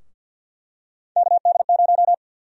Morse code, heard as a keyed tone, about 51 wpm, 700 Hz. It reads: KD0
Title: USA Call Sign Prefixes - 50wpm